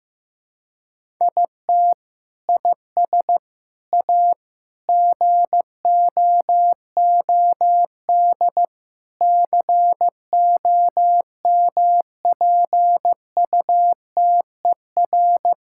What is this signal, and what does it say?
Morse code, heard as a keyed tone, about 15 wpm, 700 Hz. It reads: IT IS A GOOD COMPUTER
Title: Sentences from Top 1000 Words - Review Part 5 - 15wpm